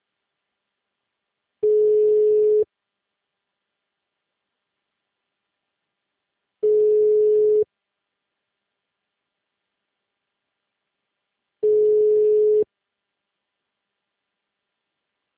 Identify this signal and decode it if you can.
Ringback tone (Europe, 425 Hz).